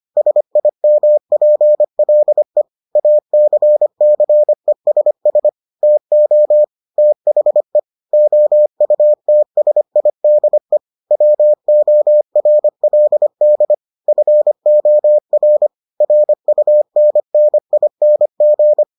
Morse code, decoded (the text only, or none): SIMPLE ACCESS TO THE OUTSIDE WORLD FOR RUNNING